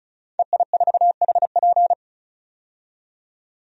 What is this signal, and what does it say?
Morse code, heard as a keyed tone, about 36 wpm, 700 Hz. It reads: EI4HP
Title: DXCC-100 Call Signs - 5 Characters - 35wpm